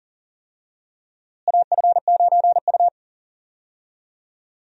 Morse code, heard as a keyed tone, about 37 wpm, 700 Hz. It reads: AF9U
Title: USA Call Signs - 4 Characters - Repeated 2x - 40wpm